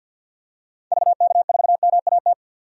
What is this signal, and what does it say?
Morse code, heard as a keyed tone, about 48 wpm, 700 Hz. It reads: VK4MAT
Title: DXCC-100 Call Signs - 6 Characters - 50wpm